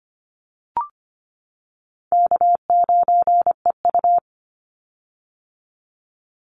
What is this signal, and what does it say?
Morse code, heard as a keyed tone, about 25 wpm, 700 Hz. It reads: K9EU
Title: Call Signs - 25wpm